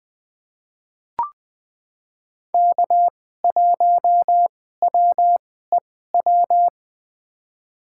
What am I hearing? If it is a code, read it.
Morse code, heard as a keyed tone, about 20 wpm, spaced as for 15 wpm, 700 Hz. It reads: K1WEW
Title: USA Call Signs - 5 Characters - 2x Character Spacing - 20wpm